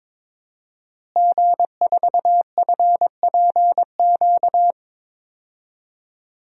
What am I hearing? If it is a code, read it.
Morse code, heard as a keyed tone, about 22 wpm, 700 Hz. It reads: G4FPQ
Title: DXCC-100 Call Signs - 5 Characters - Repeated 3x - 22wpm